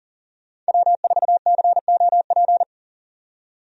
Morse code, heard as a keyed tone, about 40 wpm, 700 Hz. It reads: W4COP
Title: USA Call Signs - 5 Characters - 40wpm